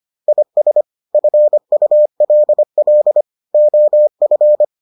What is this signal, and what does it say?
Morse code, heard as a keyed tone, about 25 wpm, 600 Hz. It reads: IS FULL OF